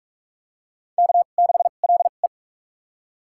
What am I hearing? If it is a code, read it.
Morse code, heard as a keyed tone, about 46 wpm, spaced as for 32 wpm, 700 Hz. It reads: K6LE
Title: USA Call Signs - 2x Character Spacing - 45wpm